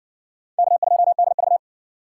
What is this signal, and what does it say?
Morse code, heard as a keyed tone, about 56 wpm, 700 Hz. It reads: K2DV